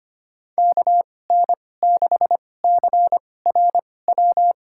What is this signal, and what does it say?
Morse code, heard as a keyed tone, about 25 wpm, spaced as for 18 wpm, 700 Hz. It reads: KN6CRW
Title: USA Call Signs - 2x Character Spacing - 25wpm